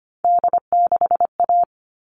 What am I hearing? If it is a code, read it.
Morse code, heard as a keyed tone, about 25 wpm, 700 Hz. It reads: D6A
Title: All Cut Numbers - 3-Digit 25wpm